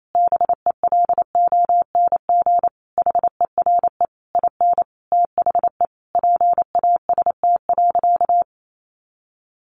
Morse code, heard as a keyed tone, about 28 wpm, 700 Hz. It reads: BELONG HERE IN THE PAST.
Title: A Sound Of Thunder 28wpm